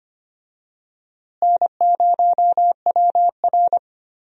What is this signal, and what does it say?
Morse code, heard as a keyed tone, about 25 wpm, 700 Hz. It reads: N0WR
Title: USA Call Signs - 4 Characters - Repeated 2x - 25wpm